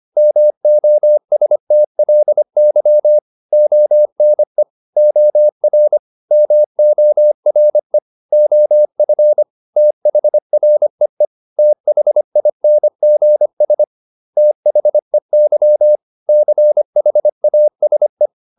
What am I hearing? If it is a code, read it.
Morse code, heard as a keyed tone, about 25 wpm, 600 Hz. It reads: MOSTLY ONE OR MORE OF THREE THINGS THEY CHASE